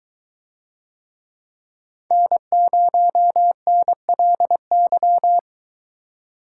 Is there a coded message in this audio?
Yes: Morse code from 2.1 to 5.4 s, heard as a keyed tone, about 22 wpm, 700 Hz. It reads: N0NLY